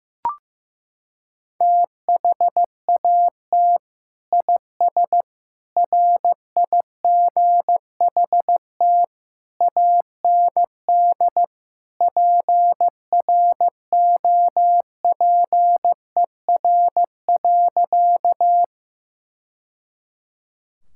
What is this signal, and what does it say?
Morse code, heard as a keyed tone, about 15 wpm, 700 Hz. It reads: THAT IS RIGHT AND PROPER.
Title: Sentences from Top 1000 Words - Review Part 5 - 15wpm